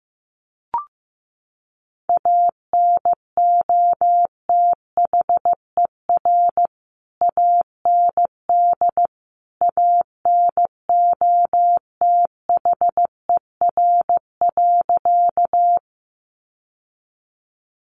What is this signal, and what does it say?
Morse code, heard as a keyed tone, about 15 wpm, 700 Hz. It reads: ANOTHER AND ANOTHER.
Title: Sentences from Top 600 Words 15wpm